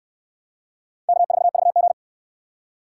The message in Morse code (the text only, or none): K4VG